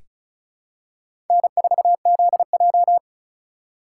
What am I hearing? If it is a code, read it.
Morse code, heard as a keyed tone, about 36 wpm, 700 Hz. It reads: N4ZJ
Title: USA Call Signs - 4 Characters - 4x Character Spacing - 35wpm